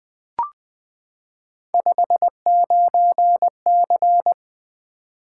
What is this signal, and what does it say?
Morse code, heard as a keyed tone, about 20 wpm, 700 Hz. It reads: 59C
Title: Three Letter-Number Combinations - 20wpm